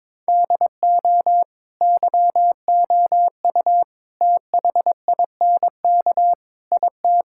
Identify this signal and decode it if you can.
Morse code, heard as a keyed tone, about 22 wpm, 700 Hz. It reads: DO YOU THINK IT
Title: Sentences from Top 1000 Words - Review Part 6 - 22wpm